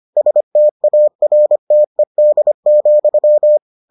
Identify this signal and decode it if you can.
Morse code, heard as a keyed tone, about 25 wpm, 600 Hz. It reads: STARTED,